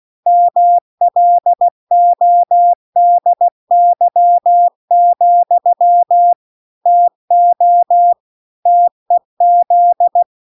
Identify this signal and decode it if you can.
Morse code, heard as a keyed tone, about 16 wpm, 700 Hz. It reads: MLODY, TO TEZ